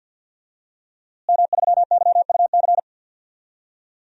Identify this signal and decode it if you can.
Morse code, heard as a keyed tone, about 52 wpm, 700 Hz. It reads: M3YUC
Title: DXCC-100 Call Signs - 5 Characters - Repeated 3x - 50wpm